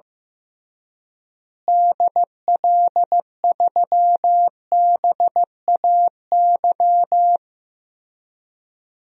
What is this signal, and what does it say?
Morse code, heard as a keyed tone, about 15 wpm, 700 Hz. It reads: DL3BAY
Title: DXCC-100 Call Signs - 6 Characters - Repeated 3x - 15wpm